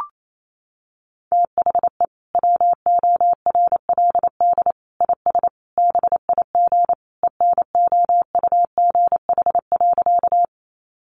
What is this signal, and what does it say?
Morse code, heard as a keyed tone, about 28 wpm, 700 Hz. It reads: THE WORLD IS BIG ENOUGH.
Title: Sentences from Top 400 Words 28wpm